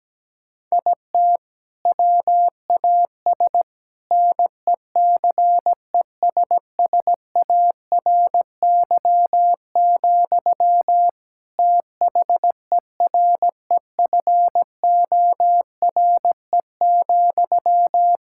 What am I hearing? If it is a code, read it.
Morse code, heard as a keyed tone, about 17 wpm, 700 Hz. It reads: IT WAS NECESSARY, THEREFORE,
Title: Sentences from Top 1600 Words 17wpm